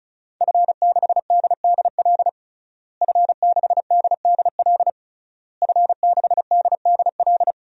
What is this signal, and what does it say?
Morse code, heard as a keyed tone, about 34 wpm, 700 Hz. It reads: F6DDL F6DDL F6DDL
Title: DXCC-100 Call Signs - 5 Characters - Repeated 3x - 35wpm